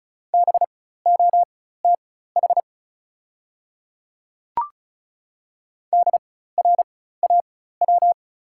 Morse code, heard as a keyed tone, about 36 wpm, spaced as for 16 wpm, 690 Hz. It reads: BOTH DRAW
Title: 4x Character Spacing - Top 300 Words - 35wpm